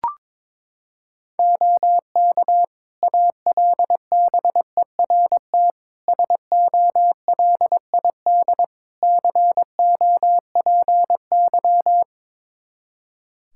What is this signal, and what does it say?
Morse code, heard as a keyed tone, about 22 wpm, 700 Hz. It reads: OK ALBERT SOLID COPY
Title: Practice QSO - Exchange 3 - More Info - 22wpm